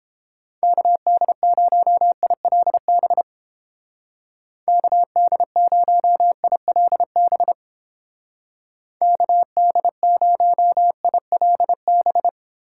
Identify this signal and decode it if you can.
Morse code, heard as a keyed tone, about 29 wpm, 700 Hz. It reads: KD0ILB KD0ILB KD0ILB